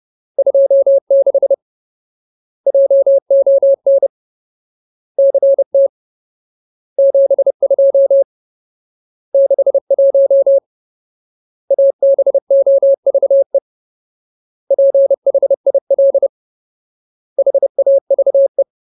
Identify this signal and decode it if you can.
Morse code, heard as a keyed tone, about 30 wpm, 550 Hz. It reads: 26 JON CT 72 61 ABOVE PHIL HAVE
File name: Ultimate Word List / 30WPM Ultimate